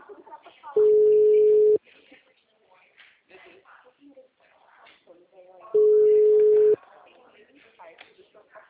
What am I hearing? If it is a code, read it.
Ringback tone (Europe, 425 Hz), with speech.